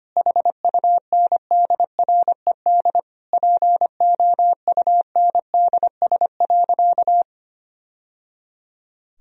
Morse code, heard as a keyed tone, about 25 wpm, 700 Hz. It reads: HUNDRED POUNDS.
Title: Sentences from Top 1000 Words - Review Part 4 - 25wpm